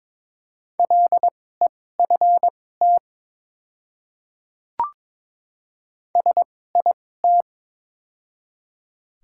Morse code, heard as a keyed tone, about 22 wpm, spaced as for 16 wpm, 690 Hz. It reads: LEFT SIT
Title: 2x Character Spacing - Top 500 Words - 22wpm